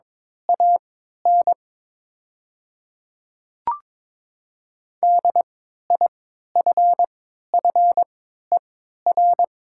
Morse code, heard as a keyed tone, about 22 wpm, spaced as for 13 wpm, 700 Hz. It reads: AN DIFFER
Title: 3x Character Spacing - Top 200 Words - 22wpm